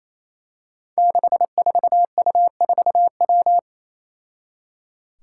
Morse code, heard as a keyed tone, about 28 wpm, 700 Hz. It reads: 64U4W
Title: Five Challenging Letter-Number Combinations 28wpm